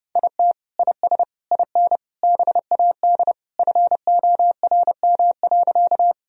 Morse code, heard as a keyed tone, about 30 wpm, 700 Hz. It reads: IT IS IN BAD FORM.
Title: Sentences from Top 1000 Words - Review Part 1 - 30wpm